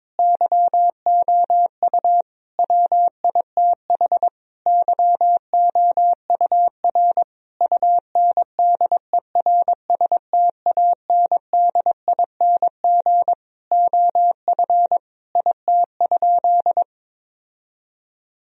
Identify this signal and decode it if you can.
Morse code, heard as a keyed tone, about 22 wpm, 700 Hz. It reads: YOU WITH YOUR UNDERSTANDING OF IT?
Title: Sentences from Top 1000 Words - Review Part 3 - 22wpm